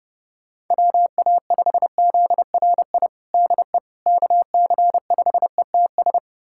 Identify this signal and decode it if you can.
Morse code, heard as a keyed tone, about 30 wpm, 700 Hz. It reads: WA5ZRI DE KC5ETS
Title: USA Callsign DE Callsign 30wpm